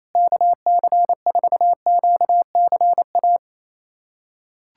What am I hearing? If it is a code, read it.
Morse code, heard as a keyed tone, about 28 wpm, 700 Hz. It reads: KC4QCA
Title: USA Callsign DE Callsign 28wpm